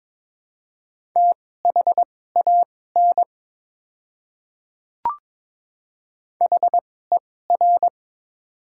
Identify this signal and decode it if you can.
Morse code, heard as a keyed tone, about 22 wpm, spaced as for 16 wpm, 700 Hz. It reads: THAN HER
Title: 2x Character Spacing - Top 200 Words - 22wpm